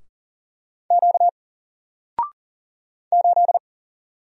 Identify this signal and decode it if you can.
Morse code, heard as a keyed tone, about 41 wpm, 700 Hz. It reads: Q 8
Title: Single Letter-Number Alt 40wpm